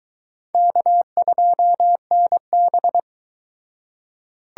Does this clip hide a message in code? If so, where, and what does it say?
Yes: Morse code from 0.5 to 3.0 s, heard as a keyed tone, about 23 wpm, 700 Hz. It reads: K2NB